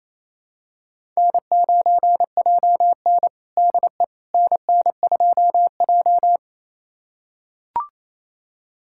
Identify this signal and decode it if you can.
Morse code, heard as a keyed tone, about 28 wpm, 700 Hz. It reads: N9JN DE NN2J
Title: USA Callsign DE Callsign - 4 Characters - 28wpm